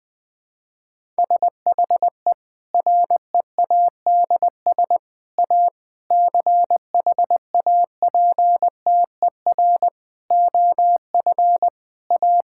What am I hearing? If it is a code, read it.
Morse code, heard as a keyed tone, about 20 wpm, 700 Hz. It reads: SHE READS A CHAPTER OF A
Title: Difficult Sentences with Hints - 20wpm